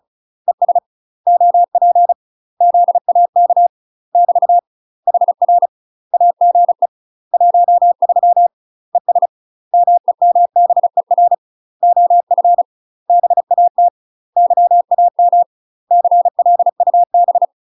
Morse code, heard as a keyed tone, about 34 wpm, 700 Hz. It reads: ES OP ZAK <BT> HR AGE 13 ES MEMBER OF BAT YAM CLUB
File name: UK MASTER QSOs TRAINING / 35WPM / UK MASTER QSOs 0021